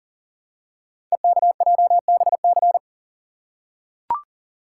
Morse code, heard as a keyed tone, about 38 wpm, 690 Hz. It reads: EKJBC
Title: Code Groups - 5 Letters - 40wpm